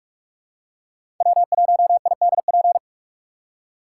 Morse code, heard as a keyed tone, about 44 wpm, 700 Hz. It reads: W1IDP
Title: USA Call Signs - 5 Characters - 6x Character Spacing - 45wpm